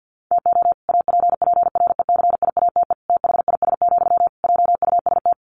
Morse code, heard as a keyed tone, about 50 wpm, 700 Hz. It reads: TO APPRECIATE THIS, JUST
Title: Sentences from Top 1300 Words 50wpm